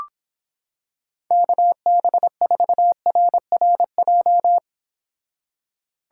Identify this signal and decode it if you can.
Morse code, heard as a keyed tone, about 26 wpm, 700 Hz. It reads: KB4RRJ